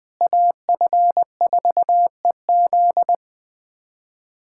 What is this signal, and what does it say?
Morse code, heard as a keyed tone, about 20 wpm, 700 Hz. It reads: AF4EZ
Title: USA Callsign DE Callsign - 5 Characters - 20wpm